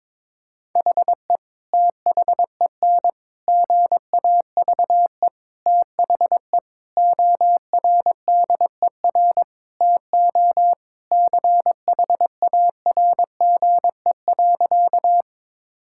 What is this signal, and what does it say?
Morse code, heard as a keyed tone, about 22 wpm, 700 Hz. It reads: HE THEN GAVE THE ORDER TO CHARGE.
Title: Sentences from Top 1000 Words - Review Part 6 - 22wpm